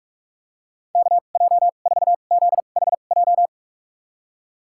Morse code, heard as a keyed tone, about 43 wpm, spaced as for 32 wpm, 700 Hz. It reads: KJ4ZHJ
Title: USA Call Signs - 6 Characters - 2x Character Spacing - 45wpm